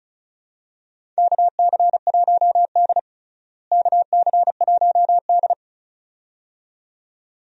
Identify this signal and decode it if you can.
Morse code, heard as a keyed tone, about 36 wpm, 700 Hz. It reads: KC1D KC1D
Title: USA Call Signs - 4 Characters - Repeated 2x - 35wpm